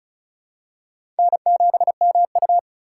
Morse code, heard as a keyed tone, about 36 wpm, 700 Hz. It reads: N7MU
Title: USA Call Signs - 2x Character Spacing - 35wpm